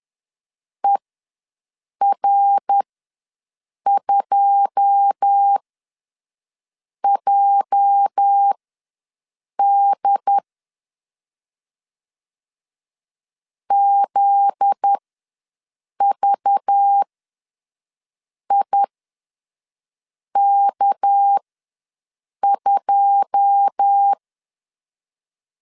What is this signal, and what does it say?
Morse code, heard as a keyed tone, about 10 wpm, spaced as for 5 wpm, 800 Hz. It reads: ER2JD ZVIK2
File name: LEZIONI / 9 2 ws 10wpm 4130